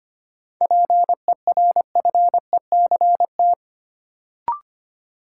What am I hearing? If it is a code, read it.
Morse code, heard as a keyed tone, about 25 wpm, 700 Hz. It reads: PERFECT
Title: Sentences from Top 1000 Words - New Word Drill - 25wpm